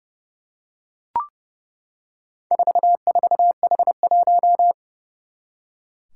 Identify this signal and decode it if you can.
Morse code, heard as a keyed tone, about 32 wpm, 700 Hz. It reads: 44H1
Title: Four Challenging Letter-Number Combinations 30wpm